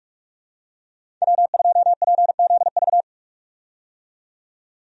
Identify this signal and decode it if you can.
Morse code, heard as a keyed tone, about 42 wpm, 700 Hz. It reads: W2PZV
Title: USA Call Signs - 5 Characters - 45wpm